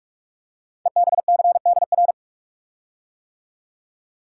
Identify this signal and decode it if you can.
Morse code, heard as a keyed tone, about 44 wpm, 690 Hz. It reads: EBCDR
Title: Code Groups - 5 Letters - 45wpm